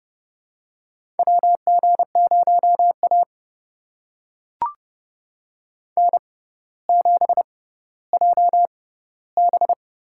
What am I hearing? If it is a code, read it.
Morse code, heard as a keyed tone, about 30 wpm, spaced as for 15 wpm, 700 Hz. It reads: WG0A N7JB
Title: USA Call Signs - 4 Characters - 6x Character Spacing - 30wpm